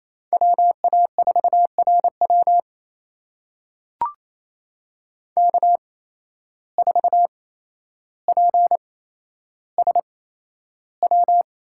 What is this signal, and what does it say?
Morse code, heard as a keyed tone, about 28 wpm, spaced as for 12 wpm, 700 Hz. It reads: WA4RW K4PSW
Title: USA Call Signs - 5 Characters - 8x Character Spacing - 28wpm